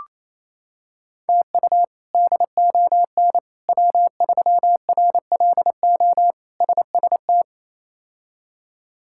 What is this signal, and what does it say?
Morse code, heard as a keyed tone, about 28 wpm, 700 Hz. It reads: TU DON W3RLO SST
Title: K1USN SST QSO Practice - 28wpm